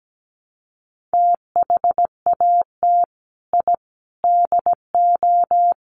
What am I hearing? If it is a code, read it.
Morse code, heard as a keyed tone, about 17 wpm, 700 Hz. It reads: THAT I DO
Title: Sets of 4 Words 17wpm